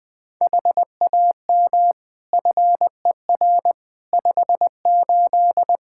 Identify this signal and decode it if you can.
Morse code, heard as a keyed tone, about 20 wpm, 700 Hz. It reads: HAM FER 58
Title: Practice QSO - Exchange 3 - More Info - 20wpm